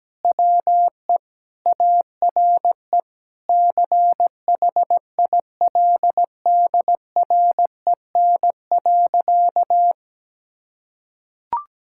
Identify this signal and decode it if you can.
Morse code, heard as a keyed tone, about 17 wpm, 700 Hz. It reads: WE ARE CHILDREN.
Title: Sentences from Top 1000 Words - Review Part 6 - 17wpm